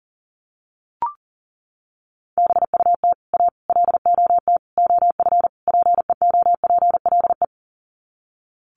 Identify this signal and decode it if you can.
Morse code, heard as a keyed tone, about 40 wpm, 700 Hz. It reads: BUT A LOT OF PEOPLE
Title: Sets of 5 Words Encore2 40wpm